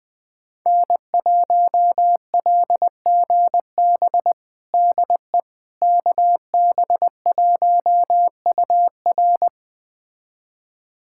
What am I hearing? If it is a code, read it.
Morse code, heard as a keyed tone, about 20 wpm, 700 Hz. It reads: N1LGB DE KB1UR
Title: USA Callsign DE Callsign - 5 Characters - 20wpm